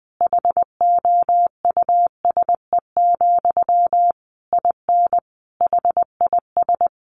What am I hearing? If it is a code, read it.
Morse code, heard as a keyed tone, about 20 wpm, 700 Hz. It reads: HOUSE, IN HIS
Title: Sentences from Top 300 Words 20wpm